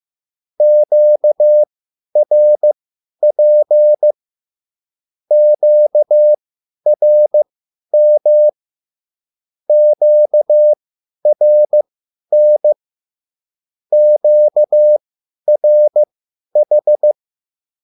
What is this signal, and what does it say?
Morse code, heard as a keyed tone, about 15 wpm, spaced as for 10 wpm, 600 Hz. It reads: QRP QRM QRN QRH
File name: QSL / CW lesson